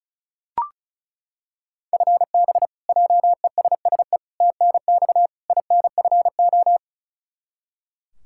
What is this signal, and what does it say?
Morse code, heard as a keyed tone, about 36 wpm, 700 Hz. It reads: FB JESSE TNX INFO
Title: K1USN SST QSO Practice 35wpm